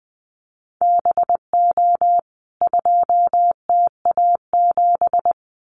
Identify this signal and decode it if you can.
Morse code, heard as a keyed tone, about 20 wpm, 700 Hz. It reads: BO 2TA7
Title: CWT Exchange 20wpm